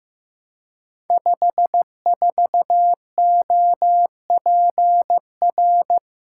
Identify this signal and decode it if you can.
Morse code, heard as a keyed tone, about 15 wpm, 700 Hz. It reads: 54OPR
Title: Code Groups - 5 Letter:Numbers - 15wpm